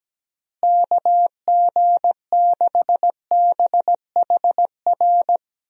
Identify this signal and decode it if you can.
Morse code, heard as a keyed tone, about 17 wpm, 700 Hz. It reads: KG6BHR
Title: USA Call Signs - 6 Characters - Repeated 3x - 17wpm